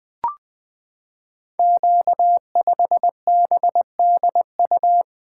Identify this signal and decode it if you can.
Morse code, heard as a keyed tone, about 20 wpm, 700 Hz. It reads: Q5BDU
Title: Code Groups - 5 Letter:Numbers - 20wpm